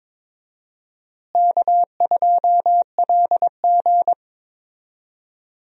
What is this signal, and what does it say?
Morse code, heard as a keyed tone, about 22 wpm, 700 Hz. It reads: K2LG
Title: USA Call Signs - 4 Characters - Repeated 2x - 22wpm